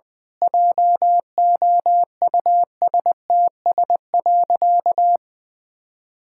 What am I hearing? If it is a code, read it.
Morse code, heard as a keyed tone, about 20 wpm, 700 Hz. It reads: JOUSTS.
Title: Pangram Challenge v2 - 20wpm